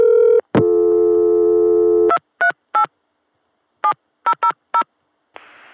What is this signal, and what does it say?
DTMF, heard as telephone touch tones, tones of about 80 ms. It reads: 238 7000